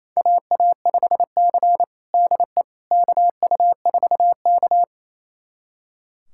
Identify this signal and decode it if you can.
Morse code, heard as a keyed tone, about 28 wpm, 700 Hz. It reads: AA5C DE KU4K
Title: USA Callsign DE Callsign - 4 Characters - 28wpm